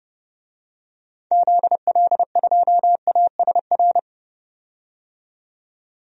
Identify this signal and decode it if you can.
Morse code, heard as a keyed tone, about 30 wpm, 700 Hz. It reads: ZL2ASR
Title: DXCC-100 Call Signs - 6 Characters - 30wpm